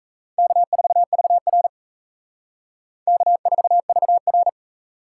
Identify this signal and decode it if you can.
Morse code, heard as a keyed tone, about 33 wpm, 700 Hz. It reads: K4VR K4VR
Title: Speed Racing - USA 1x2 Call Signs 28wpm